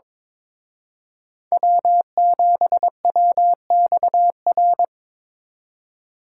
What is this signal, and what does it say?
Morse code, heard as a keyed tone, about 25 wpm, 700 Hz. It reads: W7WXR
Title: Speed Racing - USA 1x3 Call Signs 22wpm